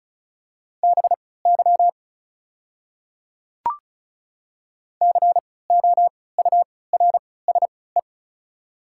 Morse code, heard as a keyed tone, about 36 wpm, spaced as for 20 wpm, 700 Hz. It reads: BY COURSE
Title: 3x Character Spacing - Top 500 Words - 35wpm